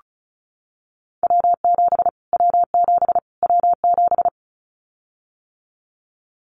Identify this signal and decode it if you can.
Morse code, heard as a keyed tone, about 34 wpm, 700 Hz. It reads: W7 W7 W7
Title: Callsign Prefixes for Top 25 DXCC - Repeated 3x - 35wpm